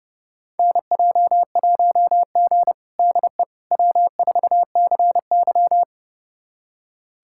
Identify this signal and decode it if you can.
Morse code, heard as a keyed tone, about 30 wpm, 700 Hz. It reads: NJ1G DE W4CY
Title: USA Callsign DE Callsign - 4 Characters - 30wpm